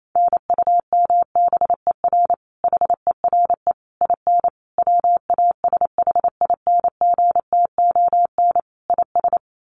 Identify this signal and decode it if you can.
Morse code, heard as a keyed tone, about 28 wpm, 700 Hz. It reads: NUMBER HERE IN WASHINGTON IS
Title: Sets of 5 Words Encore2 28wpm